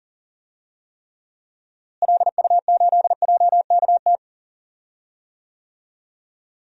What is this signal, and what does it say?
Morse code, heard as a keyed tone, about 40 wpm, 700 Hz. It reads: LU8JKT
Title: DXCC-100 Call Signs - 6 Characters - Repeated 3x - 40wpm